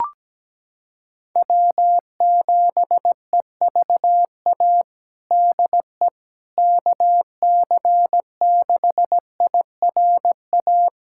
Morse code, heard as a keyed tone, about 17 wpm, 700 Hz. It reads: W7EVA DE KC6IRA
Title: USA Callsign DE Callsign 17wpm